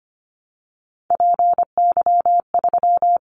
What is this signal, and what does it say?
Morse code, heard as a keyed tone, about 25 wpm, 700 Hz. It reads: PY3
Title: Callsign Prefixes for Top 50 DXCC - 25wpm